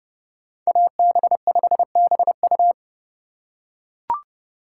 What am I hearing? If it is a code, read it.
Morse code, heard as a keyed tone, about 29 wpm, 700 Hz. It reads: AB5BU